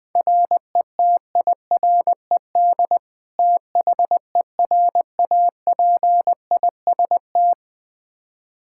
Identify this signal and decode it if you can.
Morse code, heard as a keyed tone, about 20 wpm, 700 Hz. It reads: RETIRED THERAPIST
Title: Practice QSO - Exchange 3 - More Info - 20wpm